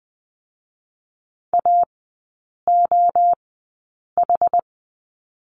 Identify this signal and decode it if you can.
Morse code, heard as a keyed tone, about 20 wpm, spaced as for 8 wpm, 690 Hz. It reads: AOH
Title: 060 Lesson 11e - Letter h - Three Characters 20wpm